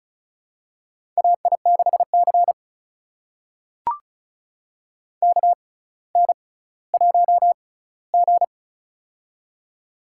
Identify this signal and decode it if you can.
Morse code, heard as a keyed tone, about 36 wpm, spaced as for 18 wpm, 700 Hz. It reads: AI6C KN1G
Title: USA Call Signs - 4 Characters - 6x Character Spacing - 35wpm